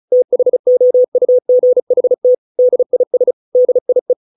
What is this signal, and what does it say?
Morse code, heard as a keyed tone, about 34 wpm, 500 Hz. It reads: THOUGHT DIS DIE